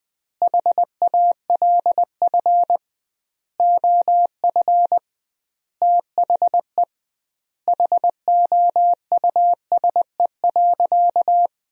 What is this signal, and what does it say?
Morse code, heard as a keyed tone, about 20 wpm, 700 Hz. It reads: HALF OF THE HOUSE.
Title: 2x Word-Spacing - Sentences from Top 400 Words - 20wpm